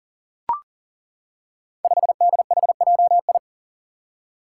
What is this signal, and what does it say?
Morse code, heard as a keyed tone, about 39 wpm, 700 Hz. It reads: 5DHJI
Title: Five Challenging Letter-Number Combinations 40wpm